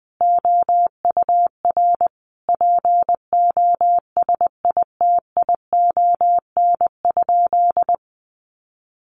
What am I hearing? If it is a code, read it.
Morse code, heard as a keyed tone, about 20 wpm, 700 Hz. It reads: OUR POSITION?
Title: Sentences from Top 900 Words 20wpm